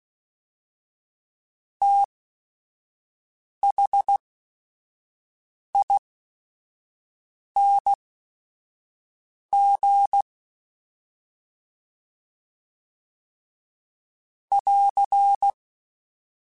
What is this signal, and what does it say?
Morse code, heard as a keyed tone, about 16 wpm, spaced as for 5 wpm, 800 Hz. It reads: THING <AR>
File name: Special INGs / Special INGs-mp3s / VerySlowToFaster Ing @5